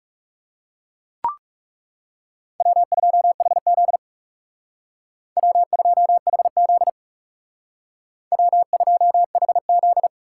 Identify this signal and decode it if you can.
Morse code, heard as a keyed tone, about 42 wpm, 700 Hz. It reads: W2HZ W2HZ W2HZ